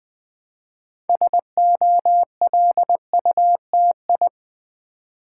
Morse code, heard as a keyed tone, about 20 wpm, 700 Hz. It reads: SOLUTI
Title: Build A Word - 3 Letters At A Time - 8-Letter Words - 20wpm